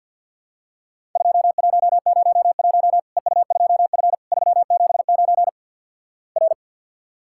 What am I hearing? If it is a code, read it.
Morse code, heard as a keyed tone, about 53 wpm, 700 Hz. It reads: 2101 EU2F 379 R
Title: Worked All Europe DX Contest - QTC Exchange with Repeats - 50wpm